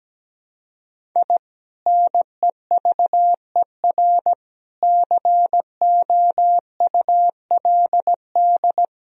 Morse code, heard as a keyed tone, about 17 wpm, 700 Hz. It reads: I NEVER COULD
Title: Sentences from Top 1000 Words - Review Part 1 - 17wpm